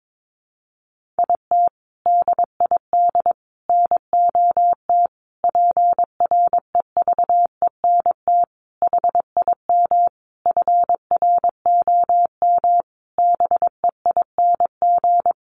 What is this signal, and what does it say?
Morse code, heard as a keyed tone, about 22 wpm, 700 Hz. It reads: IT DID NOT PREVENT HIM FROM BEING
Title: Sentences from Top 1500 Words 22wpm